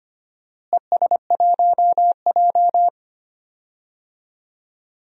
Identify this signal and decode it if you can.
Morse code, heard as a keyed tone, about 25 wpm, 700 Hz. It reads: ES1J
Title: DXCC-100 Call Signs - 4 Characters - Repeated 2x - 25wpm